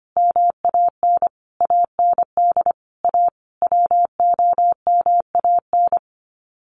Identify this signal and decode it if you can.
Morse code, heard as a keyed tone, about 25 wpm, 700 Hz. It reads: MAN AND A WOMAN
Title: Sets of 5 Words Encore 25wpm